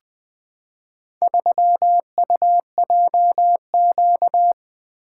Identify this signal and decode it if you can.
Morse code, heard as a keyed tone, about 21 wpm, 700 Hz. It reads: 3UJQ
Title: Four Letter-Number Combinations - 20wpm